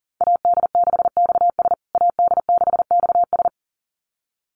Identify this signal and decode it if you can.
Morse code, heard as a keyed tone, about 40 wpm, 700 Hz. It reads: AD6XS AD6XS
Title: USA Call Signs Repeated 2x 40wpm